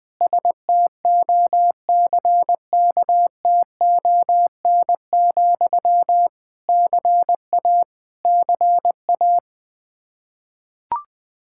Morse code, heard as a keyed tone, about 20 wpm, 700 Hz. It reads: STOCKTON, CA CA
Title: Practice QSO - Exchange 1 - Beginning - 20wpm